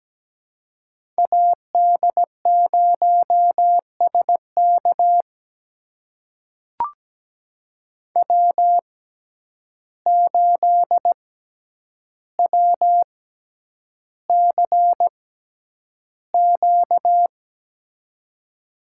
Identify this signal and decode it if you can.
Morse code, heard as a keyed tone, about 17 wpm, spaced as for 9 wpm, 700 Hz. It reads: AD0SK W8WCQ
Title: USA Call Signs - 5 Characters - 6x Character Spacing - 17wpm